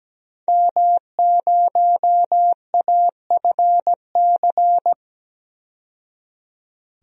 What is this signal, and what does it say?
Morse code, heard as a keyed tone, about 17 wpm, 700 Hz. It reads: M0AFC
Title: DXCC-100 Call Signs - 5 Characters - 17wpm